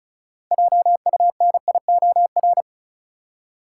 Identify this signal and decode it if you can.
Morse code, heard as a keyed tone, about 36 wpm, 700 Hz. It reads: JUNIOR
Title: Six Letter Words - 35wpm